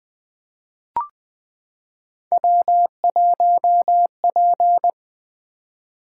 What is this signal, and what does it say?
Morse code, heard as a keyed tone, about 20 wpm, 700 Hz. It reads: W1P
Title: Three Letter-Number Combinations - 20wpm